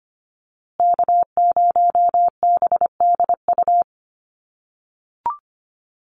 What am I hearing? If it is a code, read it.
Morse code, heard as a keyed tone, about 25 wpm, 700 Hz. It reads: K0BDU
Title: USA Call Signs Repeated 3x 25wpm